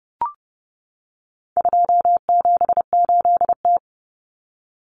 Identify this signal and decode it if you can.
Morse code, heard as a keyed tone, about 30 wpm, 700 Hz. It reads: 278T
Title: Common Cut Numbers - 4-Digit 30wpm